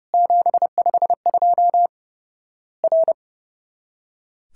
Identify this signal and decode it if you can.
Morse code, heard as a keyed tone, about 29 wpm, 700 Hz. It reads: 752 R
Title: Worked All Europe DX Contest - QTC Exchange - 30wpm